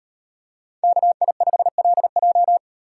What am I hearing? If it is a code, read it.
Morse code, heard as a keyed tone, about 38 wpm, 700 Hz. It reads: KI5LJ